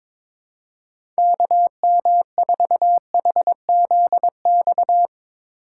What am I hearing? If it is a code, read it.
Morse code, heard as a keyed tone, about 22 wpm, 700 Hz. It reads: KM4HZX
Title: USA Call Signs - 6 Characters - 2x Character Spacing - 22wpm